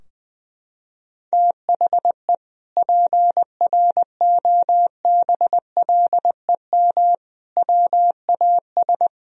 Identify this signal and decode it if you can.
Morse code, heard as a keyed tone, about 20 wpm, 700 Hz. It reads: THE PROBLEM WAS
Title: Sentences from Top 1000 Words - Review Part 2 - 20wpm